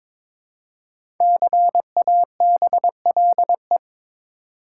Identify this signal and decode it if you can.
Morse code, heard as a keyed tone, about 22 wpm, 700 Hz. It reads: CABLE
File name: 5 Letter Words - First 3 Letters Given - 22wpm